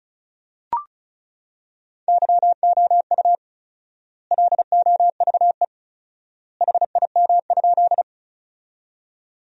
Morse code, heard as a keyed tone, about 36 wpm, 700 Hz. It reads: YOU LOVE HIM?
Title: 4x Word-Spacing - Sentences from Top 400 Words - 35wpm